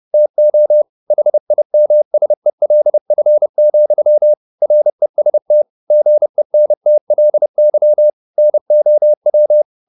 Morse code, heard as a keyed tone, about 30 wpm, 600 Hz. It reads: TO HIMSELF, REST GENTLY NOW
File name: the old man and the sea / the old man and the sea 1 30wpm